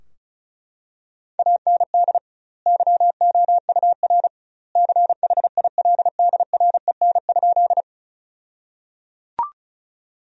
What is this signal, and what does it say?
Morse code, heard as a keyed tone, about 36 wpm, 700 Hz. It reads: AND YOUR CHILDREN?
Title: 2x Word-Spacing - Sentences from Top 300 Words - 35wpm